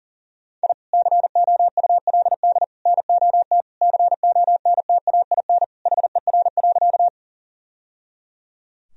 Morse code, heard as a keyed tone, about 37 wpm, 700 Hz. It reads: I COULD NOT CONTAIN HER.
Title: Sentences from Top 1000 Words - Review Part 2 - 40wpm